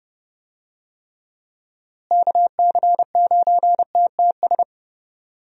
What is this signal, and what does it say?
Morse code, heard as a keyed tone, about 31 wpm, 700 Hz. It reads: KC9TTS
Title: USA Call Signs - 6 Characters - Repeated 3x - 30wpm